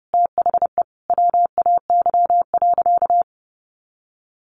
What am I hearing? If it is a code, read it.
Morse code, heard as a keyed tone, about 30 wpm, 700 Hz. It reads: THE WAY.
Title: Sentences from Top 900 Words 30wpm